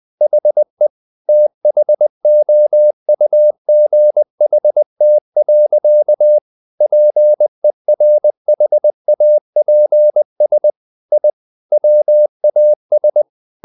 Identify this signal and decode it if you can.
Morse code, heard as a keyed tone, about 20 wpm, 600 Hz. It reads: HE THOUGHT. PERHAPS I WAS
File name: the old man and the sea part 2 20wpm